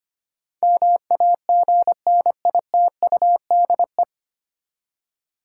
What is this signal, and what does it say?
Morse code, heard as a keyed tone, about 25 wpm, 700 Hz. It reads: MAGNITUDE
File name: Nine Letter Words - 25wpm